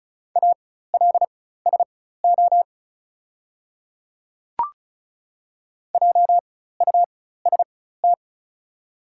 Morse code, heard as a keyed tone, about 36 wpm, spaced as for 16 wpm, 700 Hz. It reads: ALSO JUST
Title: 4x Character Spacing - Top 200 Words - 35wpm